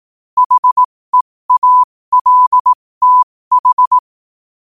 Morse code, heard as a keyed tone, about 18 wpm, spaced as for 15 wpm, 1000 Hz. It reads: HEALTH
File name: qotdfiles / 15wpm